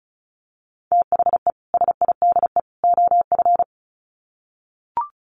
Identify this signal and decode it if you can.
Morse code, heard as a keyed tone, about 34 wpm, 700 Hz. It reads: THE SIDE OF
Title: Sets of 3 Words 35wpm